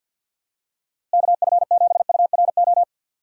Morse code, heard as a keyed tone, about 53 wpm, 700 Hz. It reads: KF7URO
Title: USA Call Signs - 6 Characters - Repeated 2x - 50wpm